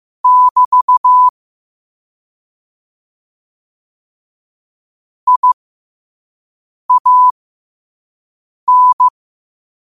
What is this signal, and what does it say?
Morse code, heard as a keyed tone, about 15 wpm, spaced as for 5 wpm, 1000 Hz. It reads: <BT> IAN
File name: qotdfiles / 5wpm